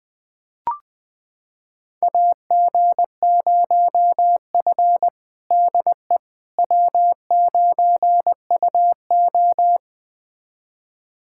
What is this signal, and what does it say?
Morse code, heard as a keyed tone, about 20 wpm, 700 Hz. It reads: AG0F DE W9UO
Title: USA Callsign DE Callsign - 4 Characters - 20wpm